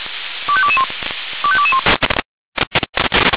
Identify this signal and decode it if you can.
Five-tone selective call (ZVEI), heard as quick runs of five tones, tones of about 70 ms. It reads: 25221 25221